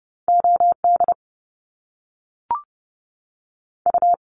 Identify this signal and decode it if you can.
Morse code, heard as a keyed tone, about 30 wpm, 700 Hz. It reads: OD U
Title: Top Two Letter Combinations in Words 30wpm